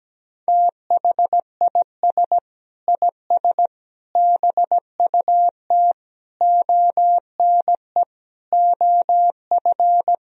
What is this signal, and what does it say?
Morse code, heard as a keyed tone, about 17 wpm, 700 Hz. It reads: THIS IS BUT ONE OF
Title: Sentences from Top 1000 Words - Review Part 2 - 17wpm